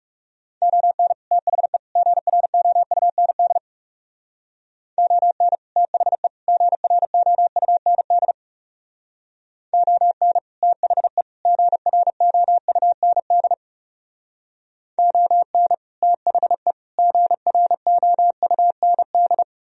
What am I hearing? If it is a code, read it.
Morse code, heard as a keyed tone, about 36 wpm, 700 Hz. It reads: ON THE GROUND ON THE GROUND ON THE GROUND ON THE GROUND
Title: Speed-Racing - Sets of 3 Words 30wpm